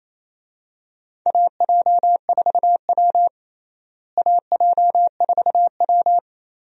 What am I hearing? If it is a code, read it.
Morse code, heard as a keyed tone, about 28 wpm, 700 Hz. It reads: AJ4W AJ4W
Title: USA Call Signs - 4 Characters - Repeated 2x - 28wpm